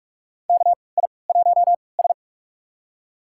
Morse code, heard as a keyed tone, about 41 wpm, spaced as for 25 wpm, 700 Hz. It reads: KI1S
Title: USA Call Signs - 3x Character Spacing - 45wpm